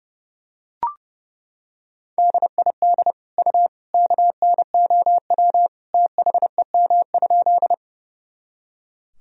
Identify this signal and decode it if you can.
Morse code, heard as a keyed tone, about 30 wpm, 700 Hz. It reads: DID U KNOW THEM?
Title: Sentences from Top 100 Words and Abbreviations 30wpm